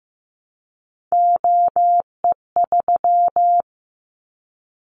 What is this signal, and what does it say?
Morse code, heard as a keyed tone, about 15 wpm, 700 Hz. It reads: OE3
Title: Callsign Prefixes for Top 100 DXCC - 15wpm